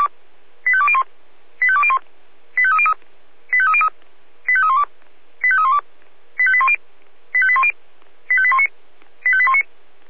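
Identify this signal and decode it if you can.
Five-tone selective call (ZVEI), heard as quick runs of five tones, tones of about 70 ms. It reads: 85201 85201 85303 85303 85312 85312 85810 85810 85819 85819
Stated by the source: ZVEI five-tone call received by SDR.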